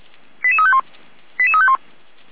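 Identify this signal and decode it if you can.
Five-tone selective call (ZVEI), heard as quick runs of five tones, tones of about 70 ms. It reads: 88351 88351
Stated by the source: ZVEI five-tone selective call received on SDR.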